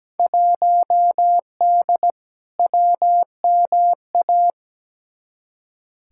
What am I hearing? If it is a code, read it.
Morse code, heard as a keyed tone, about 17 wpm, 700 Hz. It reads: 1D WMA
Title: Field Day Exchanges - 17wpm